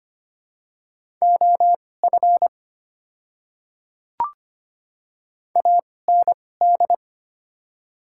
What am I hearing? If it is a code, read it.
Morse code, heard as a keyed tone, about 25 wpm, spaced as for 18 wpm, 700 Hz. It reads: OF AND
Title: 2x Character Spacing - Top 100 Words - 25wpm